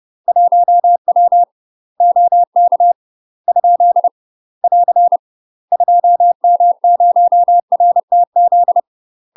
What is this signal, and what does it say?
Morse code, heard as a keyed tone, about 30 wpm, 700 Hz. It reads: 1W OK ? <AR> 2M0RTZ
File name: UK MASTER QSOs TRAINING / 30WPM / UK MASTER QSOs 0044